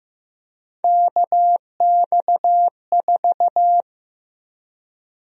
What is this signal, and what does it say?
Morse code, heard as a keyed tone, about 15 wpm, 700 Hz. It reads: KX4
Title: USA Call Sign Prefixes - 15wpm